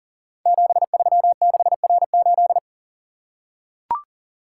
Morse code, heard as a keyed tone, about 38 wpm, 690 Hz. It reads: 736R8
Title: Code Groups - 5 Letter:Numbers - 40wpm